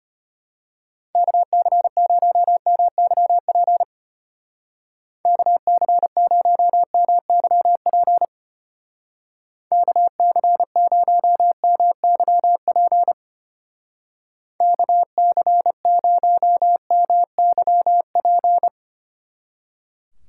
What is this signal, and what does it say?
Morse code, heard as a keyed tone, about 32 wpm, 700 Hz. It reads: KC0MYP KC0MYP KC0MYP KC0MYP